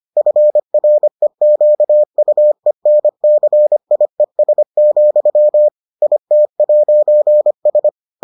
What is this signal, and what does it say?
Morse code, heard as a keyed tone, about 25 wpm, 600 Hz. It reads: FREQUENCIES, IT'S